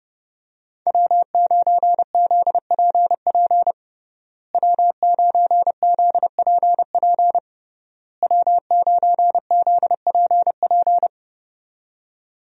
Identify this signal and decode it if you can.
Morse code, heard as a keyed tone, about 30 wpm, 700 Hz. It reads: W9ZPP W9ZPP W9ZPP
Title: USA Call Signs - 5 Characters - Repeated 3x - 30wpm